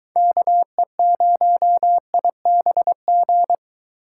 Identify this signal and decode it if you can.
Morse code, heard as a keyed tone, about 21 wpm, 700 Hz. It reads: KE0IBG